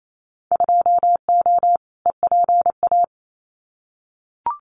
Morse code, heard as a keyed tone, about 28 wpm, 700 Hz. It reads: 2O EPA
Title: Winter Field Day Exchanges Encore 28wpm